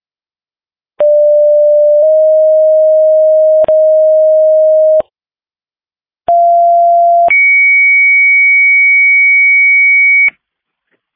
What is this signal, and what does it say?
Two-tone page, heard as a one-second tone, then a three-second tone, about 590 then 620 Hz, about 690 then 2250 Hz. It is heard from 1.0 to 10.3 s.